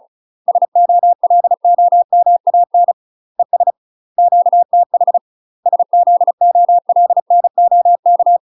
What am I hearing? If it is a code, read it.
Morse code, heard as a keyed tone, about 34 wpm, 700 Hz. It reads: SOLOMAN ES QTH SZOLNOK
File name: UK MASTER QSOs TRAINING / 35WPM / UK MASTER QSOs 0005